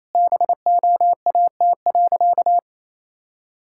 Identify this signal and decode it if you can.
Morse code, heard as a keyed tone, about 28 wpm, 700 Hz. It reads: BOAT.
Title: Sentences from Top 1000 Words - Review Part 6 - 28wpm